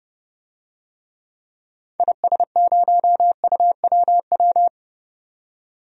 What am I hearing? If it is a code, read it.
Morse code, heard as a keyed tone, about 32 wpm, 700 Hz. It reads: IS0UWW
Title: DXCC-100 Call Signs - 6 Characters - 30wpm